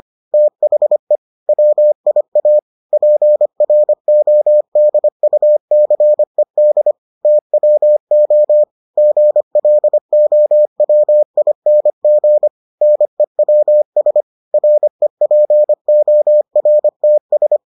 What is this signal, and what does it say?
Morse code, heard as a keyed tone, about 25 wpm, 600 Hz. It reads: THE WIA PRODUCED TWO GLOWING NEWS REPORTS